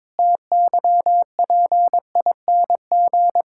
Morse code, heard as a keyed tone, about 22 wpm, 700 Hz. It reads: TYPING
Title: Pangram Challenge v2 - 22wpm